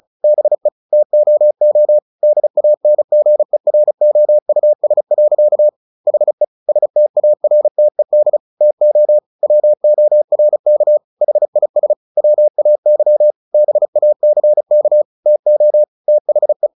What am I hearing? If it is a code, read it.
Morse code, heard as a keyed tone, about 34 wpm, 600 Hz. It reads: BE TOO DANGEROUS. HE STARTED TO WORK HIS WAY BACK TO THE
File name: the old man and the sea 2 35wpm